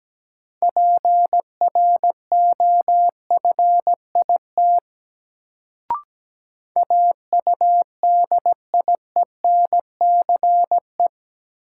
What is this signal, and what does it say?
Morse code, heard as a keyed tone, about 17 wpm, 700 Hz. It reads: PROFIT AUDIENCE
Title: Sentences from Top 800 Words - New Word Drill - 17wpm